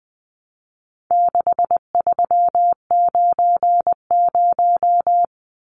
Morse code, heard as a keyed tone, about 20 wpm, 700 Hz. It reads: 6390
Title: Numbers - 4-Digit 20wpm